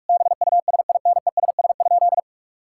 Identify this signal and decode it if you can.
Morse code, heard as a keyed tone, about 41 wpm, 700 Hz. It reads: BUSINESS?
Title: Sentences from Top 1000 Words - Review Part 5 - 45wpm